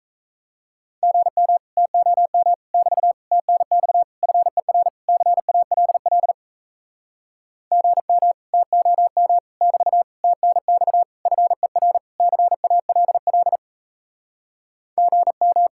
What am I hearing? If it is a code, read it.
Morse code, heard as a keyed tone, about 36 wpm, 700 Hz. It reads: GM TOM <BT> TNX FER CALL GM TOM <BT> TNX FER CALL GM